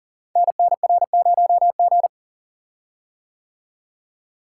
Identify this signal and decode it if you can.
Morse code, heard as a keyed tone, about 40 wpm, 690 Hz. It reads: NNR0G
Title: Code Groups - 5 Letter:Numbers - 40wpm